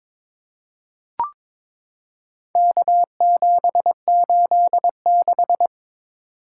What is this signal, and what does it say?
Morse code, heard as a keyed tone, about 22 wpm, 700 Hz. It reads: K786
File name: practice / 22wpm / Four Letter-Number Combinations - 22wpm